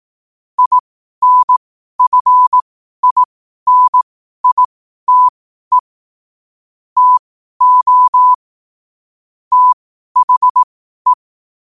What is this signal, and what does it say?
Morse code, heard as a keyed tone, about 18 wpm, spaced as for 12 wpm, 1000 Hz. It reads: INFINITE TO THE
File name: qotdfiles / 12wpm